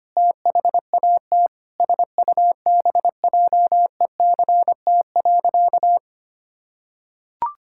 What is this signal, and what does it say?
Morse code, heard as a keyed tone, about 25 wpm, 700 Hz. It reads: THAT SUBJECT.
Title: Sentences from Top 1600 Words 25wpm